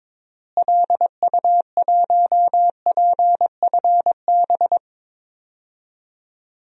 Morse code, heard as a keyed tone, about 22 wpm, 700 Hz. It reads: LU1PFB
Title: DXCC-100 Call Signs - 6 Characters - Repeated 2x - 22wpm